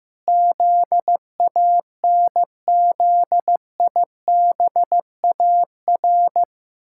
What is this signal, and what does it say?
Morse code, heard as a keyed tone, about 15 wpm, 700 Hz. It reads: ZANZIBAR
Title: Pangram Challenge v2 - 15wpm